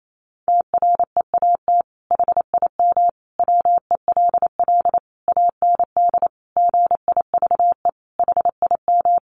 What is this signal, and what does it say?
Morse code, heard as a keyed tone, about 28 wpm, 700 Hz. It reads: TREAT HIM WELL AND GIVE HIM
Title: Sentences from Top 1300 Words 28wpm